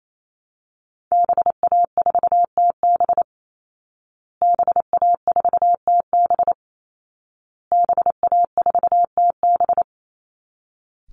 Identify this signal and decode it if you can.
Morse code, heard as a keyed tone, about 28 wpm, 700 Hz. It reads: BA4TB BA4TB BA4TB
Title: Call Signs Repeated 2x - 28wpm